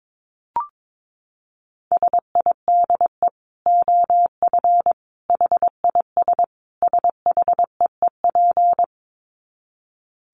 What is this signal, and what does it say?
Morse code, heard as a keyed tone, about 22 wpm, 700 Hz. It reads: SIDE OF HIS SHEEP
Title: Dit Challenge 22wpm